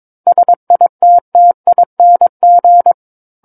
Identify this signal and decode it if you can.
Morse code, heard as a keyed tone, about 22 wpm, 700 Hz. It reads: SITTING
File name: 06-Chapter-06 22 WPM